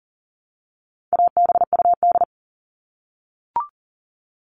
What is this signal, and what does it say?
Morse code, heard as a keyed tone, about 40 wpm, 700 Hz. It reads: ABUD
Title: All Cut Numbers - 4-Digit 40wpm